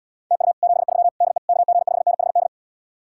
Morse code, heard as a keyed tone, about 80 wpm, 700 Hz. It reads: TU 73 DE KG5NHD